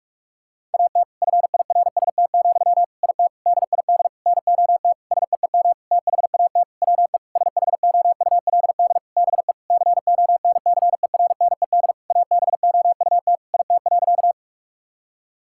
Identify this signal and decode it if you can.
Morse code, heard as a keyed tone, about 46 wpm, 700 Hz. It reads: AT FIRST, IT DID NOT SEEM THAT WE SHOULD BE CONCERNED ABOUT IT.
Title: Sentences from Top 1000 Words - Review Part 6 - 45wpm